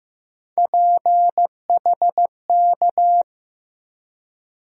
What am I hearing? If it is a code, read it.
Morse code, heard as a keyed tone, about 15 wpm, 700 Hz. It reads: PHK
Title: USA Call Sign Postfixes - 3 Letters 15wpm